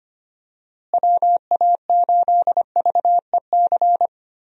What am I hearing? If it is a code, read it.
Morse code, heard as a keyed tone, about 25 wpm, 700 Hz. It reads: WA8VEC
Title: USA Call Signs - 6 Characters - Repeated 2x - 25wpm